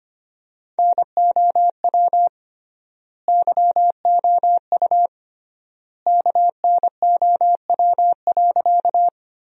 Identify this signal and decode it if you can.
Morse code, heard as a keyed tone, about 25 wpm, 700 Hz. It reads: NOW YOU KNOW.
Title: 3x Word-Spacing - Sentences from Top 100 Words - 25wpm